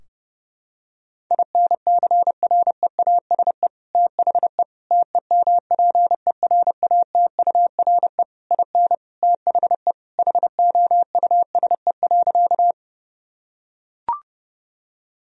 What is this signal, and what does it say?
Morse code, heard as a keyed tone, about 30 wpm, 700 Hz. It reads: INCREASE THE TEMPERATURE IN THE HOUSE.
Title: Sentences from Top 1000 Words - Review Part 4 - 30wpm